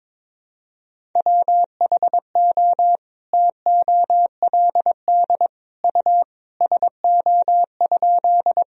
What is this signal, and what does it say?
Morse code, heard as a keyed tone, about 22 wpm, 700 Hz. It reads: WHO TOLD U SO?
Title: Sentences from Top 400 Words and Abbreviations 22wpm